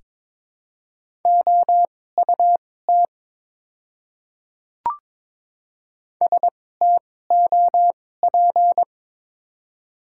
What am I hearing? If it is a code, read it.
Morse code, heard as a keyed tone, about 22 wpm, spaced as for 16 wpm, 700 Hz. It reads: OUT STOP
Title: 2x Character Spacing - Top 500 Words - 22wpm